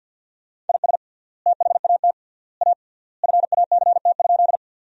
Morse code, heard as a keyed tone, about 47 wpm, 700 Hz. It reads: IS THAT A FACT?
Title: 3x Word-Spacing - Sentences from Top 500 Words - 50wpm